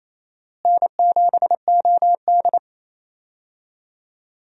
Morse code, heard as a keyed tone, about 28 wpm, 700 Hz. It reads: N7OD
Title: USA Call Signs - 4 Characters - Repeated 3x - 28wpm